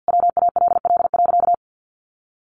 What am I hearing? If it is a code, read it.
Morse code, heard as a keyed tone, about 50 wpm, 700 Hz. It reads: WALL.
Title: Sentences from Top 1500 Words 50wpm